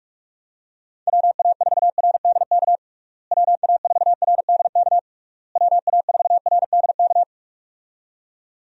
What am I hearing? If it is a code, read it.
Morse code, heard as a keyed tone, about 46 wpm, 700 Hz. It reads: WA4RDK WA4RDK WA4RDK
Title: USA Call Signs - 6 Characters - Repeated 3x - 45wpm